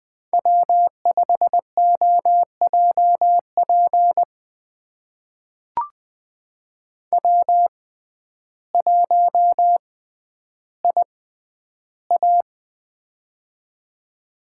Morse code, heard as a keyed tone, about 20 wpm, spaced as for 11 wpm, 700 Hz. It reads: W5OJP W1IA
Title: USA Call Signs - 6x Character Spacing - 20wpm